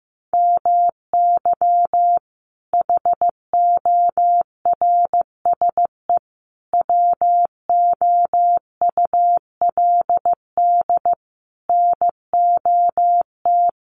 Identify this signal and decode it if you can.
Morse code, heard as a keyed tone, about 15 wpm, 700 Hz. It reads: MY HORSE WOULD NOT
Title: Sentences from Top 1400 Words 15wpm